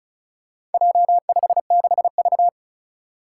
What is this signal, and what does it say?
Morse code, heard as a keyed tone, about 36 wpm, 700 Hz. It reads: J56V
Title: Four Challenging Letter-Number Combinations 35wpm